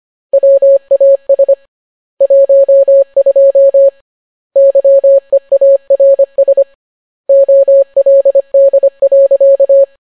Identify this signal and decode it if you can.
Morse code, heard as a keyed tone, about 25 wpm, 550 Hz. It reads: WAS 12 YEARS OLD.